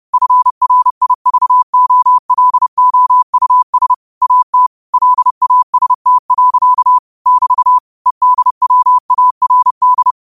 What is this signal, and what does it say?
Morse code, heard as a keyed tone, about 30 wpm, 1000 Hz. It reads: FRIVOLOUS AT LAST. <BT> EDWARD